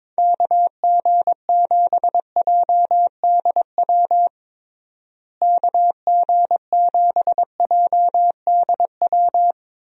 Morse code, heard as a keyed tone, about 22 wpm, 700 Hz. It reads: KG7JDW KG7JDW
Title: USA Call Signs - 6 Characters - Repeated 2x - 22wpm